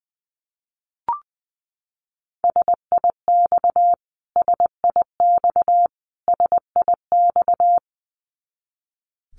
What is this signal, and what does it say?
Morse code, heard as a keyed tone, about 20 wpm, 700 Hz. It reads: SIX SIX SIX
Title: Top 500 Words Repeated 20wpm